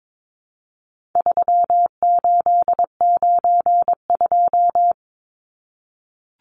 Morse code, heard as a keyed tone, about 22 wpm, 700 Hz. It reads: 3892
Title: Numbers - 4-Digit 22wpm